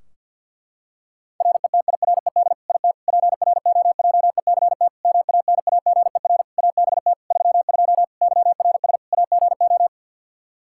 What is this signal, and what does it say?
Morse code, heard as a keyed tone, about 49 wpm, 700 Hz. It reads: RETIRED IT PROJECT MANAGER ABT 32 YRS AGO
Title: Practice QSO - Exchange 3 - More Info - 50wpm